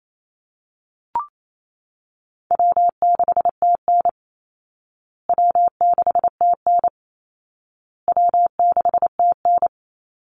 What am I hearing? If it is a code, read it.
Morse code, heard as a keyed tone, about 28 wpm, 700 Hz. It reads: W6TN W6TN W6TN
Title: Easy Call Signs Repeated 3x - 28wpm